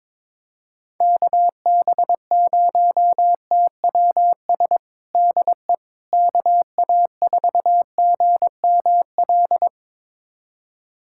Morse code, heard as a keyed tone, about 22 wpm, 700 Hz. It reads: KB0TWS DE KA4GML
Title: USA Callsign DE Callsign - 6 Characters - 22wpm